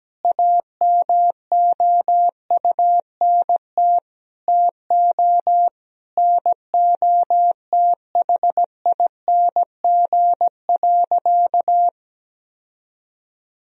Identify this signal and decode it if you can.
Morse code, heard as a keyed tone, about 17 wpm, 700 Hz. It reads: AMOUNT TO NOTHING.
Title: Sentences from Top 1000 Words - Review Part 4 - 17wpm